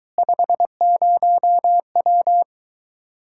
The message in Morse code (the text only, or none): 50W